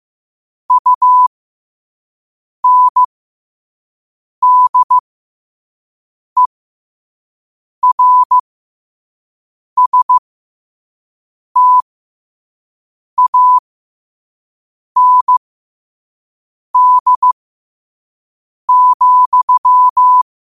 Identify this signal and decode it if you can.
Morse code, heard as a keyed tone, about 15 wpm, spaced as for 5 wpm, 1000 Hz. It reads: UNDERSTAND,
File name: qotdfiles / 5wpm